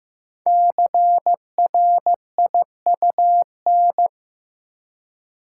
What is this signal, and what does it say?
Morse code, heard as a keyed tone, about 15 wpm, 700 Hz. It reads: CRIUN
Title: Code Groups - 5 Letters - 15wpm